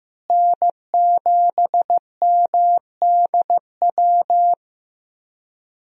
Morse code, heard as a keyed tone, about 16 wpm, 700 Hz. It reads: N7MDW
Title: USA Call Signs - 5 Characters - Repeated 2x - 15wpm